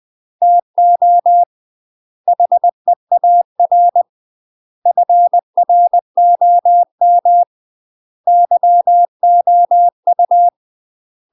Morse code, heard as a keyed tone, about 20 wpm, 700 Hz. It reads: TO HEAR FROM YOU
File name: UK MASTER QSOs TRAINING / 20WPM / UK MASTER QSOs 0064